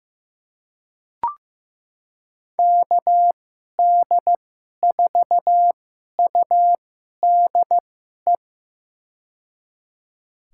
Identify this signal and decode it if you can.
Morse code, heard as a keyed tone, about 15 wpm, spaced as for 11 wpm, 700 Hz. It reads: KD4UDE
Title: USA Call Signs - 6 Characters - 2x Character Spacing - 15wpm